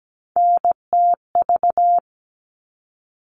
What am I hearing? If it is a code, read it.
Morse code, heard as a keyed tone, about 17 wpm, 700 Hz. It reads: NTV
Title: All Cut Numbers - 3-Digit 17wpm